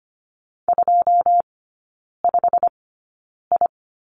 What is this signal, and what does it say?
Morse code, heard as a keyed tone, about 25 wpm, spaced as for 9 wpm, 690 Hz. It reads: 25I
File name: Lesson 15e - Number 5 - Three Characters 25wpm